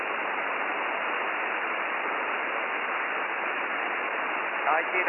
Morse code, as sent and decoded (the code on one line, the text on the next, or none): none